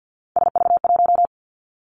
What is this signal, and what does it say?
Morse code, heard as a keyed tone, about 50 wpm, 700 Hz. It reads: SV1
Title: Callsign Prefixes for Top 50 DXCC - 50wpm